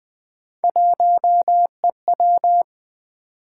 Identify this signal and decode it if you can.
Morse code, heard as a keyed tone, about 20 wpm, 700 Hz. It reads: 1EW
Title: Three Letter-Number Combinations - 20wpm